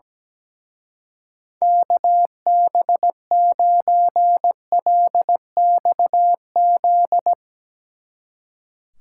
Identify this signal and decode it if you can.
Morse code, heard as a keyed tone, about 19 wpm, 700 Hz. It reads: KB9LXZ
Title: Speed Racing - USA Call Signs 17wpm